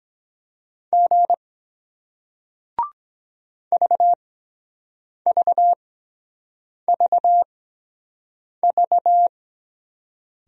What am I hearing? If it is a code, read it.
Morse code, heard as a keyed tone, about 23 wpm, spaced as for 7 wpm, 700 Hz. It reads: G VVVV